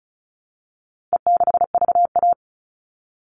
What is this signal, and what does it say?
Morse code, heard as a keyed tone, about 34 wpm, 700 Hz. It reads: E6VA
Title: All Cut Numbers - 4-Digit 35wpm